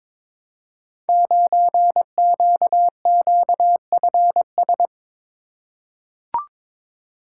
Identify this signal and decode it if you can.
Morse code, heard as a keyed tone, about 23 wpm, 700 Hz. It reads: 9QQFS
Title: Code Groups - 5 Letter:Numbers - 22wpm